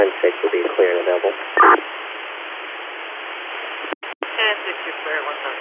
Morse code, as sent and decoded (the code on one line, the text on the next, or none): none